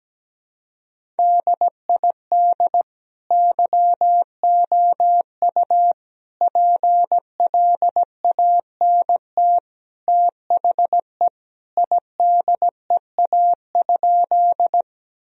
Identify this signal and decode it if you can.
Morse code, heard as a keyed tone, about 17 wpm, 700 Hz. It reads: DID YOU PLANT THE IDEA?
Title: Sentences from Top 1000 Words - Review Part 1 - 17wpm